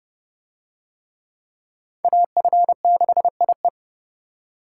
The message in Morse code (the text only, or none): AF6IE